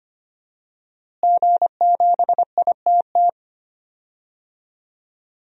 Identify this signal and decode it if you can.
Morse code, heard as a keyed tone, about 25 wpm, 700 Hz. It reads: G7ITT
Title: DXCC-100 Call Signs - 5 Characters - Repeated 2x - 25wpm